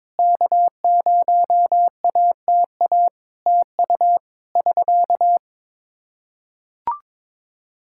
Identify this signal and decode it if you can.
Morse code, heard as a keyed tone, about 21 wpm, 700 Hz. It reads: K0ATA TU <SK>
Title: Practice QSO - Exchange 4 - Ending - 22wpm